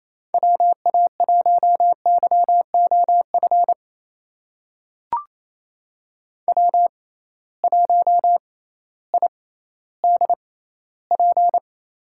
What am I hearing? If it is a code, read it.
Morse code, heard as a keyed tone, about 28 wpm, spaced as for 15 wpm, 700 Hz. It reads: WA1YOF W1IDP
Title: USA Call Signs - 6x Character Spacing - 28wpm